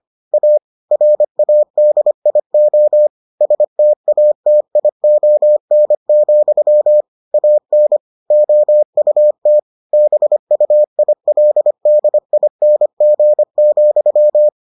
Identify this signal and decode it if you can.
Morse code, heard as a keyed tone, about 25 wpm, 600 Hz. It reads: A RADIO STATION, AN OUT BUILDING,